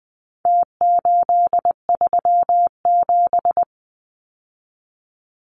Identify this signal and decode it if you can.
Morse code, heard as a keyed tone, about 20 wpm, 700 Hz. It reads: T837
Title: Numbers - Contest Style - 4-Digit 20wpm